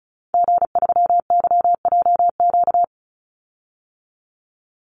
Morse code, heard as a keyed tone, about 34 wpm, 700 Hz. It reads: G3YJQ
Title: Call Signs Repeated 3x - 35wpm